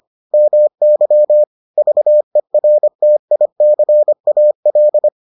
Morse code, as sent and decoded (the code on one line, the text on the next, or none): -- -.-- / ...- . .-. - .. -.-. .- .-..
MY VERTICAL